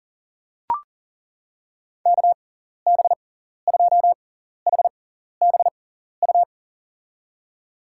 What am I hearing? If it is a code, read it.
Morse code, heard as a keyed tone, about 40 wpm, spaced as for 14 wpm, 700 Hz. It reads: KB2HBU
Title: USA Call Signs - 6x Character Spacing - 40wpm